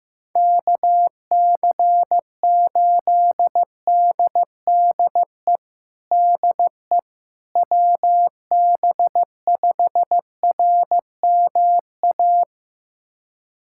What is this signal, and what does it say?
Morse code, heard as a keyed tone, about 15 wpm, 700 Hz. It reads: KC8DDE DE WB5RMA
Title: USA Callsign DE Callsign - 6 Characters - 15wpm